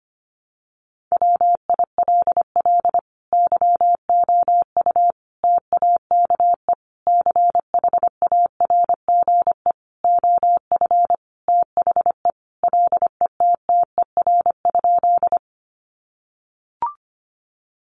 Morse code, heard as a keyed tone, about 25 wpm, 700 Hz. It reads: WILL YOU TAKE CHARGE OF THE LETTER?
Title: Sentences from Top 900 Words 25wpm